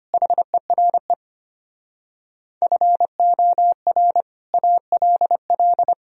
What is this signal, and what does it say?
Morse code, heard as a keyed tone, about 33 wpm, 700 Hz. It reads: HERE FOR ALL